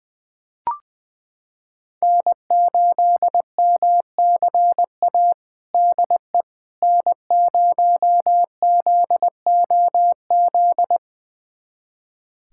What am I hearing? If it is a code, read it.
Morse code, heard as a keyed tone, about 19 wpm, 700 Hz. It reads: N8MCA DE N0ZOZ
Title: USA Callsign DE Callsign - 5 Characters - 20wpm